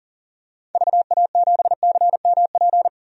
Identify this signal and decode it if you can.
Morse code, heard as a keyed tone, about 40 wpm, 700 Hz. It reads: VA7CMP
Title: DXCC-100 Call Signs - 6 Characters - 40wpm